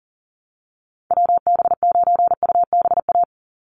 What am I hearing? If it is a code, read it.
Morse code, heard as a keyed tone, about 40 wpm, 700 Hz. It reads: WB9UBA
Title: USA Call Signs 40wpm